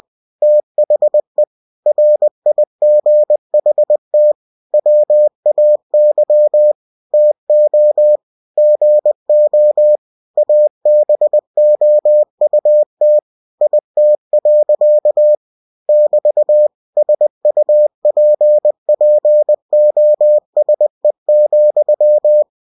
Morse code, heard as a keyed tone, about 20 wpm, 600 Hz. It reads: THE RIGHT WAY TO GO ABOUT IT. <BT> SUPPOSE,
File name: cw-books / winnie-the-pooh / winnie-20wpm-page-0036